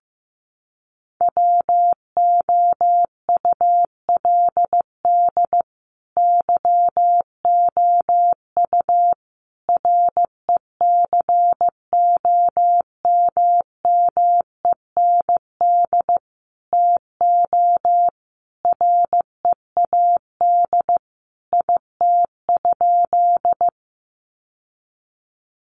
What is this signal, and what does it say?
Morse code, heard as a keyed tone, about 14 wpm, 700 Hz. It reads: WOULD YOU RECOMMEND TO READ IT?
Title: Sentences from Top 1200 Words 15wpm